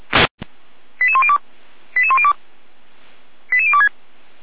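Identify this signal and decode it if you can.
Five-tone selective call (ZVEI), heard as quick runs of five tones, tones of about 70 ms. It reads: 88102 88102 88026
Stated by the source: ZVEI five-tone call received on an SDR.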